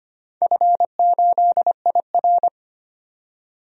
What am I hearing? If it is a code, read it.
Morse code, heard as a keyed tone, about 25 wpm, 700 Hz. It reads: F8IR
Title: DXCC-100 Call Signs - 4 Characters - 25wpm